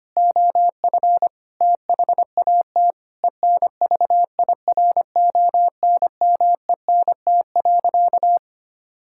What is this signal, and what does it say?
Morse code, heard as a keyed tone, about 25 wpm, 700 Hz. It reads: OF THAT ENVIRONMENT.
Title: Sentences from Top 1000 Words - Review Part 5 - 25wpm